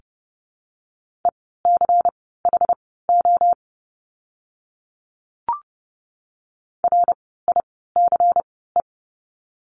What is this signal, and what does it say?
Morse code, heard as a keyed tone, about 30 wpm, spaced as for 17 wpm, 700 Hz. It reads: ECHO RICE
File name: Lesson 16f - Letter c - Words 30wpm